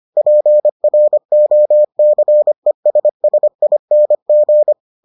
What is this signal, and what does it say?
Morse code, heard as a keyed tone, about 25 wpm, 600 Hz. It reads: PROCESSING